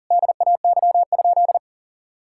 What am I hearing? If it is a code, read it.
Morse code, heard as a keyed tone, about 40 wpm, 700 Hz. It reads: DAY?
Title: Sentences from Top 1000 Words - Review Part 5 - 40wpm